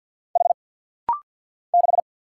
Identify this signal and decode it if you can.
Morse code, heard as a keyed tone, about 50 wpm, 700 Hz. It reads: H 6
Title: Single Letter-Number Alt - Rapid-Fire 50wpm